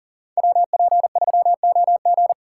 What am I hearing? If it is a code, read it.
Morse code, heard as a keyed tone, about 40 wpm, 700 Hz. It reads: WP3OG
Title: USA Call Signs - 5 Characters - 2x Character Spacing - 40wpm